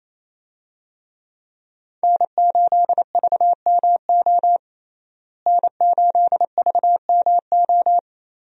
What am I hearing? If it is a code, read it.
Morse code, heard as a keyed tone, about 28 wpm, 700 Hz. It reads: N8VMO N8VMO
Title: USA Call Signs - 5 Characters - Repeated 3x - 28wpm